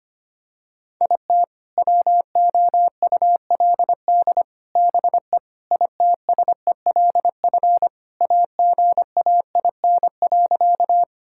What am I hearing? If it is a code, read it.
Morse code, heard as a keyed tone, about 25 wpm, 700 Hz. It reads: IT WOULD BE ITSELF AGAIN.
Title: Sentences from Top 1000 Words - Review Part 5 - 25wpm